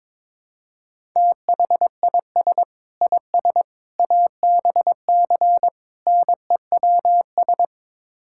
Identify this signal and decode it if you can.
Morse code, heard as a keyed tone, about 22 wpm, 700 Hz. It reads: THIS IS ABC NEWS
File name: practice / 22wpm / Build A Phrase - 5-Word Phrases - 22wpm